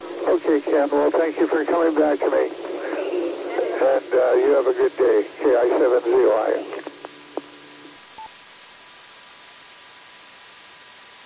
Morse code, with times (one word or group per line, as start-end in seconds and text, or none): none